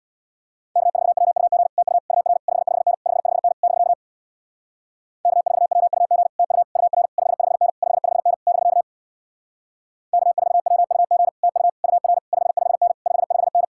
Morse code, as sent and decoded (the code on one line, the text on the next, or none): -.- ....- .-- ..- --. / - ..- / ..- .-. / ..... ..... -. / ..... ..... -. / -...-.- / -.- ....- .-- ..- --. / - ..- / ..- .-. / ..... ..... -. / ..... ..... -. / -...-.- / -.- ....- .-- ..- --. / - ..- / ..- .-. / ..... ..... -. / ..... ..... -.
K4WUG TU UR 55N 55N <BK> K4WUG TU UR 55N 55N <BK> K4WUG TU UR 55N 55N